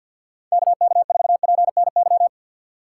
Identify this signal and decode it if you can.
Morse code, heard as a keyed tone, about 49 wpm, 690 Hz. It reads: KK4PNY
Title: USA Call Signs - 2x Character Spacing - 50wpm